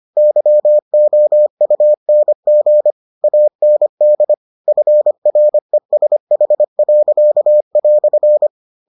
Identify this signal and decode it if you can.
Morse code, heard as a keyed tone, about 25 wpm, 600 Hz. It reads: YOUNG AND FRESH."